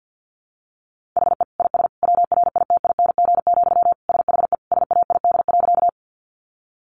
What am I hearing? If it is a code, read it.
Morse code, heard as a keyed tone, about 50 wpm, 700 Hz. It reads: HE IS WRITING, SHE SAID.
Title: Sentences from Top 800 Words 50wpm